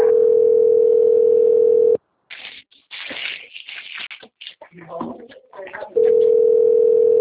Ringback tone (North America), 440 + 480 Hz.